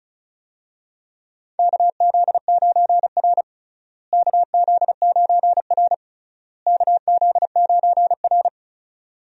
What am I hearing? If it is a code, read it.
Morse code, heard as a keyed tone, about 34 wpm, 700 Hz. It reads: KZ9R KZ9R KZ9R
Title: USA Call Signs - 4 Characters - Repeated 3x - 35wpm